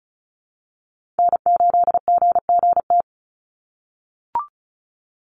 Morse code, heard as a keyed tone, about 34 wpm, 700 Hz. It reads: N8GGT
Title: USA Call Signs 35wpm